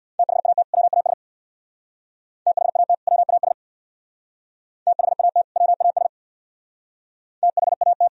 Morse code, heard as a keyed tone, about 61 wpm, 700 Hz. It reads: THAT WAS THAT WAS THAT WAS THAT
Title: Speed-Racing - Sets of 2 Words 50wpm